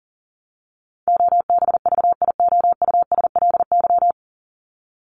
Morse code, heard as a keyed tone, about 40 wpm, 700 Hz. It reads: OBVIOUSLY
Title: LY Suffix - Top 100 Words with Base Words 40wpm